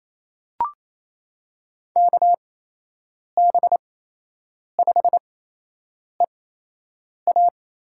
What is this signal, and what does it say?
Morse code, heard as a keyed tone, about 28 wpm, spaced as for 8 wpm, 700 Hz. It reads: KB5EA
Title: USA Call Signs - 6 Characters - 8x Character Spacing - 28wpm